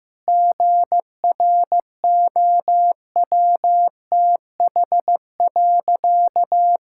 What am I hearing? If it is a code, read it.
Morse code, heard as a keyed tone, about 15 wpm, 700 Hz. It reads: GROWTH.
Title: Sentences from Top 1000 Words - Review Part 5 - 15wpm